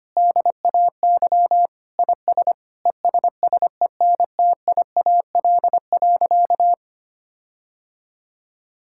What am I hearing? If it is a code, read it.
Morse code, heard as a keyed tone, about 25 wpm, 700 Hz. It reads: DAY IS ESSENTIAL.
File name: practice / 25wpm / Difficult Sentences with Hints - 25wpm